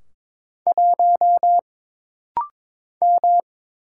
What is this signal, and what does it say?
Morse code, heard as a keyed tone, about 22 wpm, 700 Hz. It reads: 1 M
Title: Single Letter-Number Alt - Rapid-Fire 22wpm